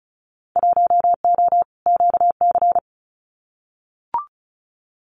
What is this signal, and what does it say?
Morse code, heard as a keyed tone, about 36 wpm, 700 Hz. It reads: 1O QC
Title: Winter Field Day Exchanges 35wpm